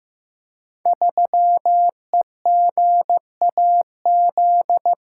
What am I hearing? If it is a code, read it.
Morse code, heard as a keyed tone, about 15 wpm, 700 Hz. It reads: 3EGAZ
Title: Code Groups - 5 Letter:Numbers - 15wpm